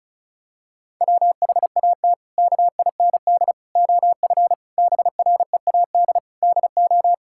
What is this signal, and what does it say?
Morse code, heard as a keyed tone, about 36 wpm, 700 Hz. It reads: WHAT KIND OF BREAD DO
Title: Sentences from Top 1600 Words 35wpm